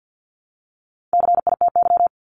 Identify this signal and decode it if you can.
Morse code, heard as a keyed tone, about 50 wpm, 700 Hz. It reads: CITY
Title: Top 500 Words 50wpm